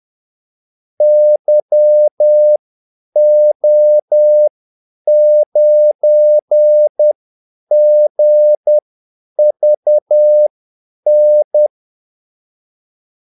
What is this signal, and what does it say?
Morse code, heard as a keyed tone, about 10 wpm, spaced as for 8 wpm, 600 Hz. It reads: YO9GVN
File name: Callsigns-10-8wpm